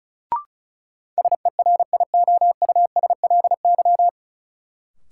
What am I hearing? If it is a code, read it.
Morse code, heard as a keyed tone, about 34 wpm, 700 Hz. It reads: SERIOUSLY
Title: Sentences from Top 1200 Words - New Word Drill Alt - 35wpm